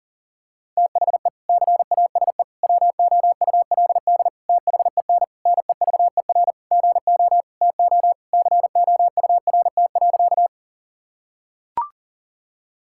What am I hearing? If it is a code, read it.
Morse code, heard as a keyed tone, about 40 wpm, 700 Hz. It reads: THE CASE WOULD THEN NEVER GO TO COURT.
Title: Sentences from Top 1600 Words 40wpm